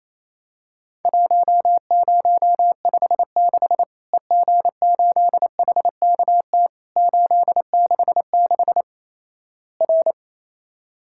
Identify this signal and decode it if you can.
Morse code, heard as a keyed tone, about 28 wpm, 700 Hz. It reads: 1056 EG8HKT 866 R
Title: Worked All Europe DX Contest - QTC Exchange - 28wpm